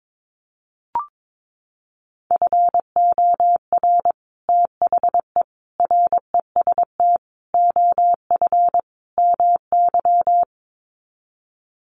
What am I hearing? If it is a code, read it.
Morse code, heard as a keyed tone, about 22 wpm, 700 Hz. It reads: FOR THE REST OF MY
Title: Sets of 5 Words 22wpm